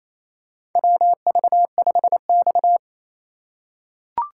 Morse code, heard as a keyed tone, about 28 wpm, 700 Hz. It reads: WV5X
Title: USA Call Signs - 4x Character Spacing - 28wpm